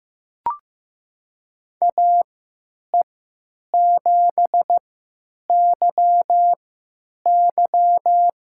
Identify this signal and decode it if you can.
Morse code, heard as a keyed tone, about 15 wpm, spaced as for 8 wpm, 700 Hz. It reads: AE7YY
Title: USA Call Signs - 5 Characters - 3x Character Spacing - 15wpm